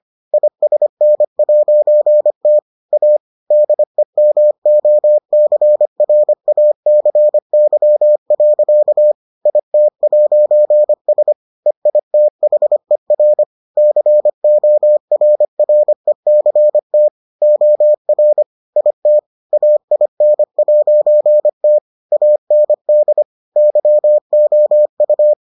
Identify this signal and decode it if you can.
Morse code, heard as a keyed tone, about 25 wpm, 600 Hz. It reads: ISN'T A DEMOCRACY. IT'S EITHER CORRECT OR IT AIN'T AND YOU